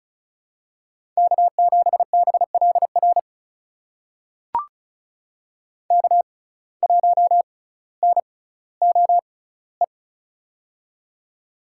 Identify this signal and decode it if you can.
Morse code, heard as a keyed tone, about 36 wpm, spaced as for 18 wpm, 700 Hz. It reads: K7BLR K1NOE
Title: USA Call Signs - 5 Characters - 6x Character Spacing - 35wpm